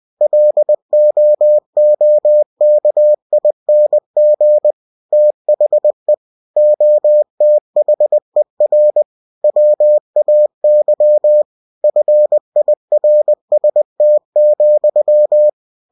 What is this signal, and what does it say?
Morse code, heard as a keyed tone, about 20 wpm, 600 Hz. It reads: LOOKING THE OTHER WAY FIRST,
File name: cw-books / winnie-the-pooh / winnie-20wpm-page-0064